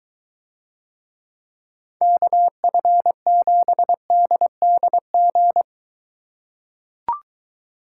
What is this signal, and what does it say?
Morse code, heard as a keyed tone, about 23 wpm, 700 Hz. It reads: KF7DDG